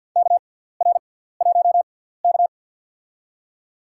Morse code, heard as a keyed tone, about 54 wpm, spaced as for 17 wpm, 700 Hz. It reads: KR1K
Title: USA Call Signs - 4 Characters - 6x Character Spacing - 50wpm